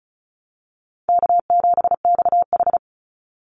Morse code, heard as a keyed tone, about 34 wpm, 700 Hz. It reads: K7XH
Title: Call Signs - 35wpm